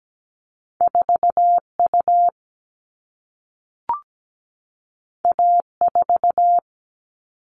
Morse code, heard as a keyed tone, about 17 wpm, 700 Hz. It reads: 4U A4
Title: All Cut Numbers - 2-Digit 17wpm